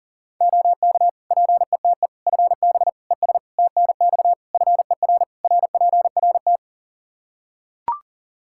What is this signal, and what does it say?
Morse code, heard as a keyed tone, about 37 wpm, 700 Hz. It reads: OK PETE FB ES TNX FER RPRT
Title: Practice QSO - Exchange 2 - Info - 40wpm